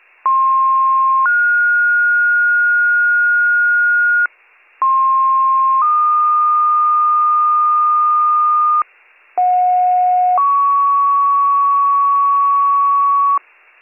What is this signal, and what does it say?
Two-tone page, heard as a one-second tone, then a three-second tone, about 1060 then 1430 Hz, about 1060 then 1220 Hz, about 730 then 1120 Hz, with noise. It is heard from 0.2 to 13.4 s.